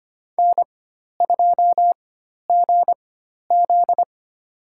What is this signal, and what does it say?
Morse code, heard as a keyed tone, about 25 wpm, spaced as for 12 wpm, 700 Hz. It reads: N2GZ
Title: USA Call Signs - 4 Characters - 4x Character Spacing - 25wpm